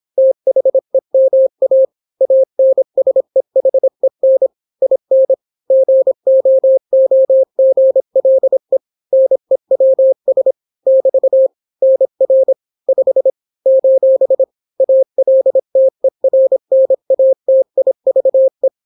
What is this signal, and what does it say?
Morse code, heard as a keyed tone, about 26 wpm, 530 Hz. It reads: THEMA ANSEHEN IN GOOGLE NEWS <BT> NR 5 : ALTERNATIVE